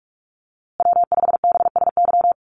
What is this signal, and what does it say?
Morse code, heard as a keyed tone, about 44 wpm, 700 Hz. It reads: W5BSY
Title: USA Call Signs Repeated 2x 45wpm